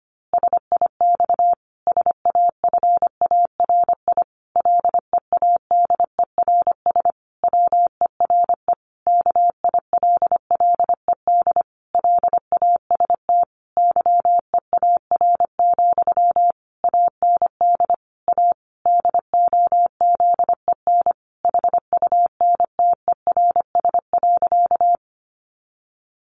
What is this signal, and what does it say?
Morse code, heard as a keyed tone, about 25 wpm, 700 Hz. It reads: SIX SAFARI LEADERS WERE KILLED LAST YEAR, AND A DOZEN HUNTERS.
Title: A Sound Of Thunder 25wpm